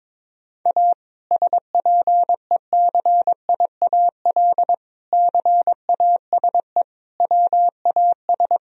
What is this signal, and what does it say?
Morse code, heard as a keyed tone, about 22 wpm, 700 Hz. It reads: A SPECIAL CASE WAS
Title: Sentences from Top 1000 Words - Review Part 3 - 22wpm